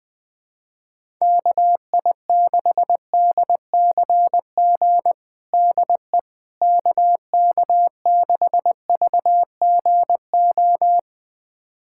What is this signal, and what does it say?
Morse code, heard as a keyed tone, about 21 wpm, 700 Hz. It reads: KI6DCG DE KK6VGO
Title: USA Callsign DE Callsign - 6 Characters - 20wpm